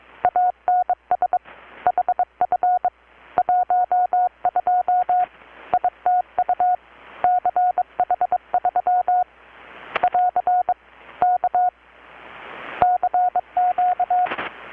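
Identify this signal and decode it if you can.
Morse code, heard as a keyed tone, about 22 wpm, 720 Hz. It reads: ANS HF 12 ITU CH3 <AR> K CQ
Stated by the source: Morse from KPH's Night of Nights XX maritime radiotelegraphy event, received in Chiapas, Mexico about 3,800 km away on a Kenwood R-600 with a grounded 10 m long-wire antenna.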